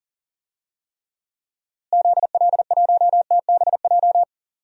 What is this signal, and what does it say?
Morse code, heard as a keyed tone, about 37 wpm, 700 Hz. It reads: ZL1TBJ
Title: DXCC-100 Call Signs - 6 Characters - Repeated 2x - 40wpm